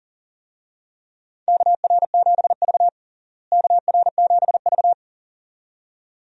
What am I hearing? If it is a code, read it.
Morse code, heard as a keyed tone, about 40 wpm, 700 Hz. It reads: KR7V KR7V
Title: USA Call Signs - 4 Characters - Repeated 2x - 40wpm